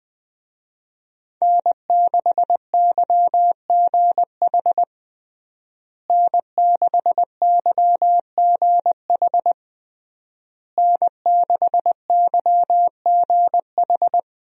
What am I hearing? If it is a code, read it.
Morse code, heard as a keyed tone, about 21 wpm, 700 Hz. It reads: N6YGH N6YGH N6YGH
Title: USA Call Signs - 5 Characters - Repeated 3x - 20wpm